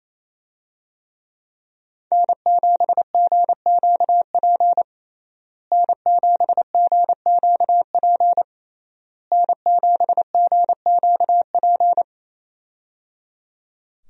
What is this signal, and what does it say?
Morse code, heard as a keyed tone, about 28 wpm, 700 Hz. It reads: N7GQP N7GQP N7GQP
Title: USA Call Signs - 5 Characters - Repeated 3x - 28wpm